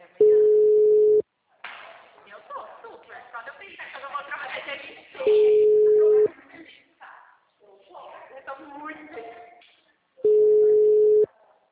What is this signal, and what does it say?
Ringback tone (Europe, 425 Hz), with speech.